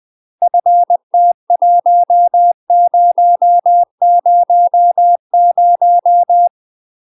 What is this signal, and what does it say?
Morse code, heard as a keyed tone, about 20 wpm, 700 Hz. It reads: FT1000
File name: UK MASTER QSOs TRAINING / 20WPM / UK MASTER QSOs 0066